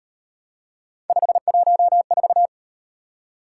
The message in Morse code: 514

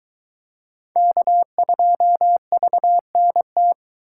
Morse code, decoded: K2VNT

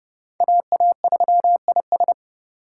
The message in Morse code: AA3IS

